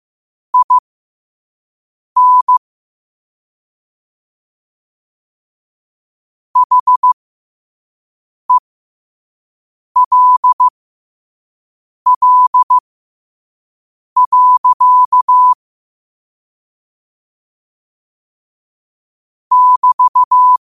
Morse code, heard as a keyed tone, about 15 wpm, spaced as for 5 wpm, 1000 Hz. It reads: IN HELL. <BT>
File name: qotdfiles / 5wpm